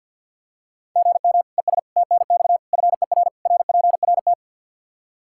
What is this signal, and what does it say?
Morse code, heard as a keyed tone, about 52 wpm, 700 Hz. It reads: GM ES TNX FER RPRT
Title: Practice QSO - Exchange 1 - Beginning - 50wpm